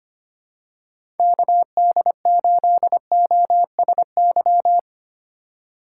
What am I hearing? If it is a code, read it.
Morse code, heard as a keyed tone, about 25 wpm, 700 Hz. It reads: KD8OSY
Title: USA Call Signs - 6 Characters - 2x Character Spacing - 25wpm